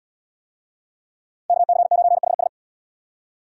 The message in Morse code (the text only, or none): KC9HS